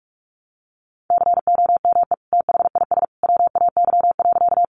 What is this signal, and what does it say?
Morse code, heard as a keyed tone, about 44 wpm, 700 Hz. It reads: COME THIS WAY.
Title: Sentences from Top 100 Words 45wpm